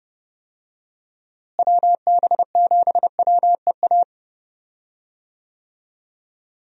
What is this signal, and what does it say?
Morse code, heard as a keyed tone, about 30 wpm, 700 Hz. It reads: WB7WEA
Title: USA Call Signs - 6 Characters - Repeated 2x - 30wpm